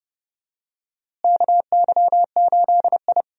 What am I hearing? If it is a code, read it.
Morse code, heard as a keyed tone, about 32 wpm, 700 Hz. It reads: KY8I
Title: USA Call Signs - 4 Characters - Repeated 3x - 30wpm